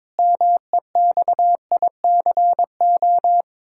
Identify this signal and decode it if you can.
Morse code, heard as a keyed tone, about 22 wpm, 700 Hz. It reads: MEXICO
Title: Pangram Challenge v2 - 22wpm